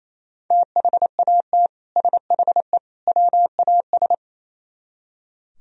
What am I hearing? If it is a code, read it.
Morse code, heard as a keyed tone, about 37 wpm, 700 Hz. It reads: THAT SHE WAS